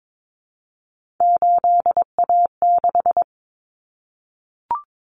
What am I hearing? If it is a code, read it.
Morse code, heard as a keyed tone, about 22 wpm, 700 Hz. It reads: 8A6
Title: Common Cut Numbers - 3-Digit 22wpm